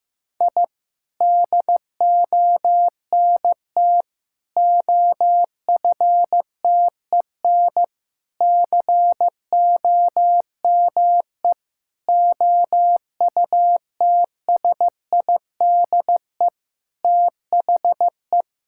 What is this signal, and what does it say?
Morse code, heard as a keyed tone, about 14 wpm, 700 Hz. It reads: I DONT OFTEN COME OUTSIDE THE
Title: Sentences from Top 1000 Words - Review Part 6 - 15wpm